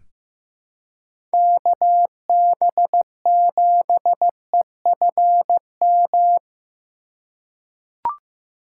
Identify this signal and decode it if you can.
Morse code, heard as a keyed tone, about 15 wpm, 700 Hz. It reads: KB7EFM
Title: USA Call Signs - 3x Character Spacing - 15wpm